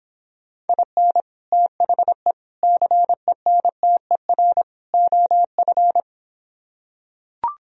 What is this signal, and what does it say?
Morse code, heard as a keyed tone, about 26 wpm, 700 Hz. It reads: IN THE CENTER OF